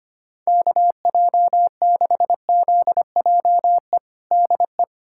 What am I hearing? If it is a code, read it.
Morse code, heard as a keyed tone, about 25 wpm, 700 Hz. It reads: KJ6ZJE DE
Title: USA Callsign DE Callsign 25wpm